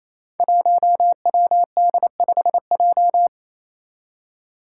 Morse code, heard as a keyed tone, about 27 wpm, 700 Hz. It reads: 1WD5J
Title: Five Challenging Letter-Number Combinations 28wpm